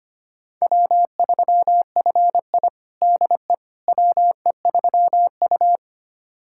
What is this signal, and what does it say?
Morse code, heard as a keyed tone, about 25 wpm, 700 Hz. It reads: W3FI DE WE3U
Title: USA Callsign DE Callsign - 4 Characters - 25wpm